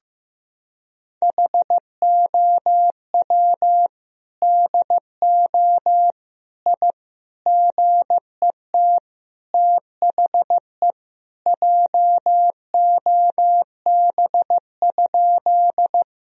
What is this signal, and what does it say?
Morse code, heard as a keyed tone, about 15 wpm, 700 Hz. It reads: HOW DO I GET THE JOB?
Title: Sentences from Top 1000 Words - Review Part 3 - 15wpm